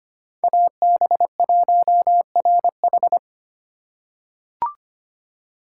Morse code, heard as a keyed tone, about 25 wpm, 700 Hz. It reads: AB1RH
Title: USA Call Signs - 3x Character Spacing - 25wpm